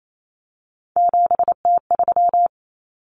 Morse code, heard as a keyed tone, about 28 wpm, 700 Hz. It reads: 7T3
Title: Common Cut Numbers - 3-Digit 28wpm